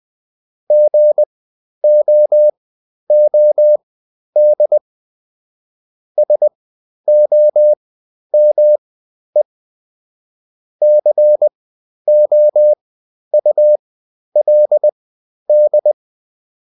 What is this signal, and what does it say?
Morse code, heard as a keyed tone, about 20 wpm, spaced as for 11 wpm, 600 Hz. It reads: GOOD SOME COULD
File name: Top-100-Words-10